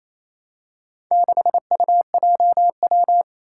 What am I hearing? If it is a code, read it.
Morse code, heard as a keyed tone, about 27 wpm, 700 Hz. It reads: 6UJW
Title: Four Challenging Letter-Number Combinations 28wpm